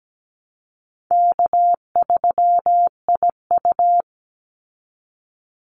Morse code, heard as a keyed tone, about 17 wpm, 700 Hz. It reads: K3IU
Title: Easy Call Signs Repeated 3x - 17wpm